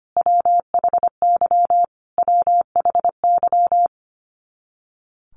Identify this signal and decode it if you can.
Morse code, heard as a keyed tone, about 25 wpm, 700 Hz. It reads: WHY WHY
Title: Top 500 Words Repeated 25wpm